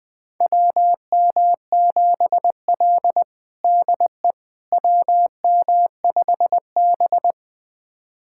Morse code, heard as a keyed tone, about 20 wpm, 700 Hz. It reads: WM7L DE WM5B
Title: USA Callsign DE Callsign - 4 Characters - 20wpm